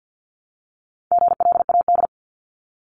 Morse code, heard as a keyed tone, about 50 wpm, 700 Hz. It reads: GLAD
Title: Multi-letter Phonemes Repeated 3x 50wpm